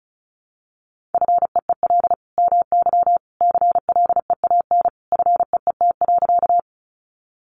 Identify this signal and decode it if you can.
Morse code, heard as a keyed tone, about 34 wpm, 700 Hz. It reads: FEEL MY CLEAN FEET.
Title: Sentences from Top 1100 Words 35wpm